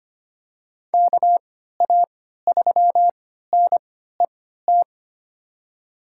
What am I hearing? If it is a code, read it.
Morse code, heard as a keyed tone, about 25 wpm, spaced as for 14 wpm, 700 Hz. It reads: KA3NET
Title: USA Call Signs - 6 Characters - 3x Character Spacing - 25wpm